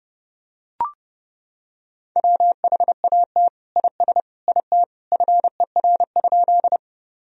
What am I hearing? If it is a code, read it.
Morse code, heard as a keyed tone, about 30 wpm, 700 Hz. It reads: WHAT IS IT FER?
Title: Sentences from Top 100 Words and Abbreviations 30wpm